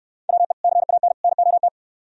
Morse code, heard as a keyed tone, about 66 wpm, 700 Hz. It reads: WE CAN NOT